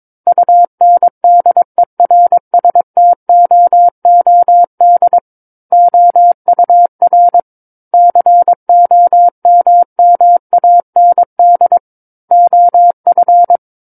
Morse code, heard as a keyed tone, about 22 wpm, 700 Hz. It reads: UNDERSTOOD OUR COMMAND OF
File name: Book One / 13-Chapter 13 - 22 WPM